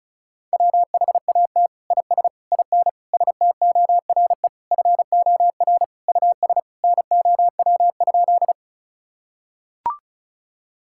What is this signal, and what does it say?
Morse code, heard as a keyed tone, about 34 wpm, 700 Hz. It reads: WHAT IS IN STORE FOR US NOW?
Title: Sentences from Top 1000 Words - Review Part 5 - 35wpm